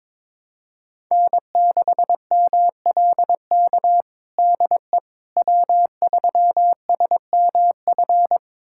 Morse code, heard as a keyed tone, about 22 wpm, 700 Hz. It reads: N6MLK DE W3SMF
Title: USA Callsign DE Callsign - 5 Characters - 22wpm